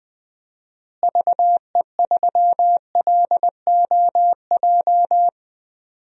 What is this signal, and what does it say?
Morse code, heard as a keyed tone, about 20 wpm, 700 Hz. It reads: VE3LOJ
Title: DXCC-100 Call Signs - 6 Characters - Repeated 3x - 20wpm